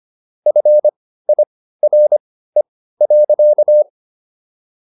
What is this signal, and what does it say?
Morse code, heard as a keyed tone, about 25 wpm, spaced as for 15 wpm, 600 Hz. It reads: FIRE.